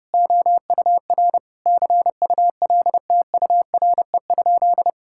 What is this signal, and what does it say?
Morse code, heard as a keyed tone, about 30 wpm, 700 Hz. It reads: OUR CULTURE?
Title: Sentences from Top 1000 Words - Review Part 6 - 30wpm